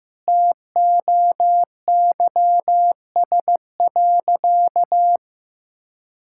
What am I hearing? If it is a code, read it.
Morse code, heard as a keyed tone, about 15 wpm, 700 Hz. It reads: TOYS.
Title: Pangram Challenge v2 - 15wpm